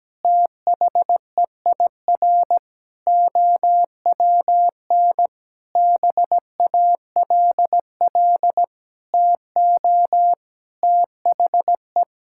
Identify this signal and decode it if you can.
Morse code, heard as a keyed tone, about 17 wpm, 700 Hz. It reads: THEIR OWN BALL TO THE
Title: Sentences from Top 1000 Words - Review Part 6 - 17wpm